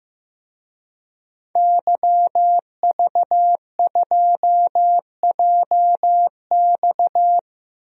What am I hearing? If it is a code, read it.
Morse code, heard as a keyed tone, about 15 wpm, 700 Hz. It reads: YV2JX
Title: DXCC-100 Call Signs - 5 Characters - Repeated 3x - 15wpm